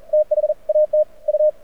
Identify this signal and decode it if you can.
Morse code, heard as a keyed tone, about 42 wpm, spaced as for 36 wpm, 590 Hz. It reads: THAT U
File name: archived / CRSnet